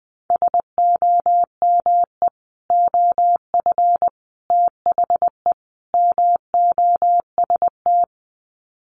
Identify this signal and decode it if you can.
Morse code, heard as a keyed tone, about 20 wpm, 700 Hz. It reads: SOME OF THE MOST
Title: Sets of 4 Words 20wpm